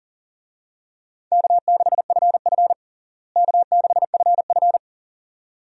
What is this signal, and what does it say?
Morse code, heard as a keyed tone, about 40 wpm, 700 Hz. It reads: K6FF K6FF
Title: USA Call Signs - 4 Characters - Repeated 2x - 40wpm